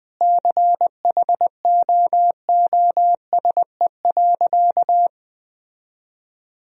Morse code, heard as a keyed tone, about 20 wpm, 700 Hz. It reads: CHOOSE.
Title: Sentences from Top 1000 Words - Review Part 4 - 20wpm